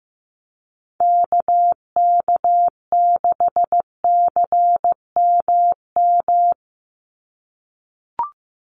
Morse code, heard as a keyed tone, about 15 wpm, 700 Hz. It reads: KK6CMM
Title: USA Call Signs 15wpm